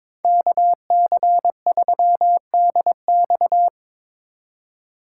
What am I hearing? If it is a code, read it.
Morse code, heard as a keyed tone, about 22 wpm, 700 Hz. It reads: KC3DX
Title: USA Call Signs - 5 Characters - 4x Character Spacing - 22wpm